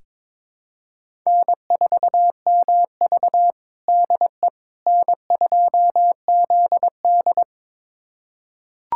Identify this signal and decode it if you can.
Morse code, heard as a keyed tone, about 22 wpm, 700 Hz. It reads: N4MV DE N2ZD
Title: USA Callsign DE Callsign - 4 Characters - 22wpm